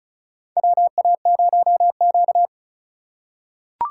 Morse code, heard as a keyed tone, about 36 wpm, 700 Hz. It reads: WA0Q
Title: USA Call Signs - 2x Character Spacing - 35wpm